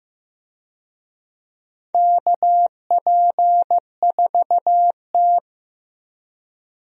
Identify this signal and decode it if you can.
Morse code, heard as a keyed tone, about 15 wpm, 700 Hz. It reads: KP4T
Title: DXCC-100 Call Signs - 4 Characters - 15wpm